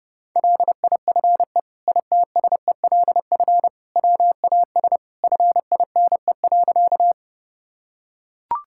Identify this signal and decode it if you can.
Morse code, heard as a keyed tone, about 30 wpm, 700 Hz. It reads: LIFE ITSELF WAS FINE.
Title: Sentences from Top 1000 Words - Review Part 5 - 30wpm